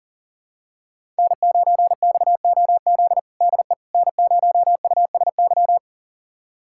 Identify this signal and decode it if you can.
Morse code, heard as a keyed tone, about 40 wpm, 700 Hz. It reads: N9XOZ DE N0USY
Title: USA Callsign DE Callsign - 5 Characters - 40wpm